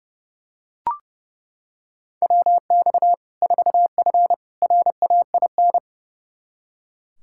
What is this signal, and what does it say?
Morse code, heard as a keyed tone, about 32 wpm, 700 Hz. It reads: WX 4F RAIN
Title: Practice QSO - Exchange 2 - Info - 30wpm